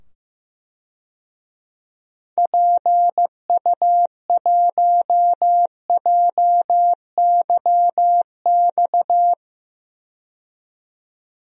Morse code, heard as a keyed tone, about 15 wpm, 700 Hz. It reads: PU1JYX
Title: DXCC-100 Call Signs - 6 Characters - 15wpm